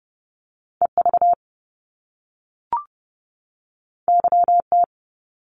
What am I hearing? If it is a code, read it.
Morse code, heard as a keyed tone, about 30 wpm, 700 Hz. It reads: EV YT
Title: Top Two Letter Combinations in Words 30wpm